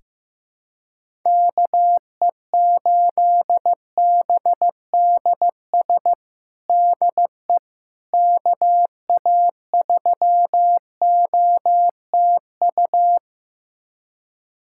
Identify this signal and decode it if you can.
Morse code, heard as a keyed tone, about 15 wpm, 700 Hz. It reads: KE8BDS DE KA3OTU
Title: USA Callsign DE Callsign 15wpm